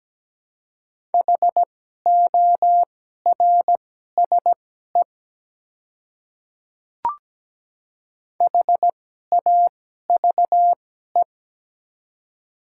Morse code, heard as a keyed tone, about 17 wpm, spaced as for 12 wpm, 700 Hz. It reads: HORSE HAVE
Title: 2x Character Spacing - Top 300 Words - 17wpm